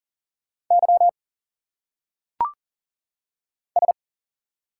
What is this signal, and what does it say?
Morse code, heard as a keyed tone, about 40 wpm, 700 Hz. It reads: Y S
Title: Single Letters - Phonetic Pronunciation 40wpm